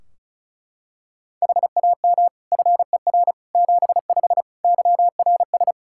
Morse code, heard as a keyed tone, about 34 wpm, 700 Hz. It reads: HAM FER 75 YRS
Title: K1USN SST QSO Practice 35wpm